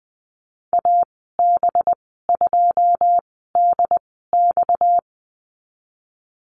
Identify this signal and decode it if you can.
Morse code, heard as a keyed tone, about 20 wpm, spaced as for 15 wpm, 700 Hz. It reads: AB2DX
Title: 231 Lesson 37g - Letter x - Call Signs 20wpm